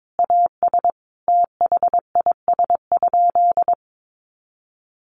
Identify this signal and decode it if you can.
Morse code, heard as a keyed tone, about 22 wpm, 700 Hz. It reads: AS THIS?
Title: Sentences from Top 400 Words 22wpm